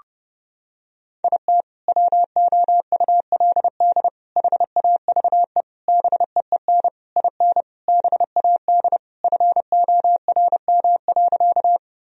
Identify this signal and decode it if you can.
Morse code, heard as a keyed tone, about 30 wpm, 700 Hz. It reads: IT WOULD HAVE BEEN IN BAD FORM.
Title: Sentences from Top 1000 Words - Review Part 1 - 30wpm